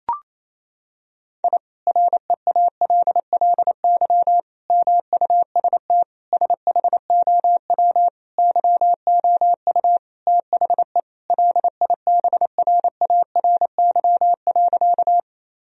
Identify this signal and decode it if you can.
Morse code, heard as a keyed tone, about 27 wpm, 700 Hz. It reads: I REALLY MUST SHOW YOU THE LIBRARY.
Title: Sentences from Top 1000 Words - Review Part 4 - 28wpm